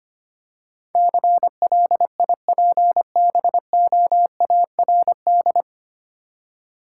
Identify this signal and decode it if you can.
Morse code, heard as a keyed tone, about 25 wpm, 690 Hz. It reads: CLIPBOARD
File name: practice / 25wpm / Easy Words with Hints - 25wpm